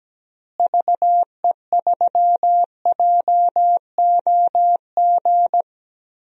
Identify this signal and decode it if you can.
Morse code, heard as a keyed tone, about 17 wpm, 700 Hz. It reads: VE3JOG
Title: DXCC-100 Call Signs - 6 Characters - Repeated 3x - 17wpm